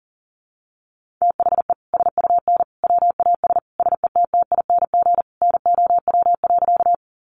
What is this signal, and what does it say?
Morse code, heard as a keyed tone, about 40 wpm, 700 Hz. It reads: THE SUN WAS SETTING NOW.
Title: Sentences from Top 1300 Words 40wpm